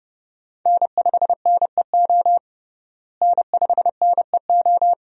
Morse code, heard as a keyed tone, about 32 wpm, 700 Hz. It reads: N5NEO N5NEO
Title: USA Call Signs - 5 Characters - Repeated 2x - 30wpm